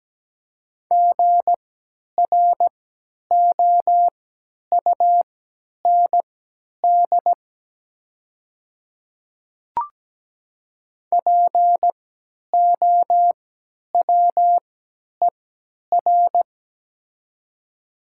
Morse code, heard as a keyed tone, about 17 wpm, spaced as for 10 wpm, 700 Hz. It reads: GROUND POWER
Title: 3x Character Spacing - Top 500 Words - 17wpm